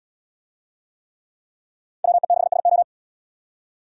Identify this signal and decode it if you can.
Morse code, heard as a keyed tone, about 69 wpm, 700 Hz. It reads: WE6SO